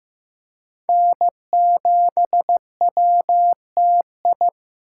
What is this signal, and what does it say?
Morse code, heard as a keyed tone, about 18 wpm, 700 Hz. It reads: N7WTI
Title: Speed Racing - USA 1x3 Call Signs 15wpm